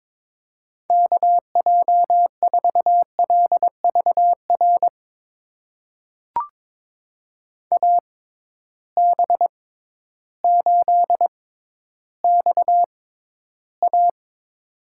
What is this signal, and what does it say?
Morse code, heard as a keyed tone, about 22 wpm, spaced as for 12 wpm, 700 Hz. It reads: KJ4LVR AB8XA
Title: USA Call Signs - 6x Character Spacing - 22wpm